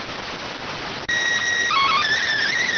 SSTV (slow-scan television), PD 290.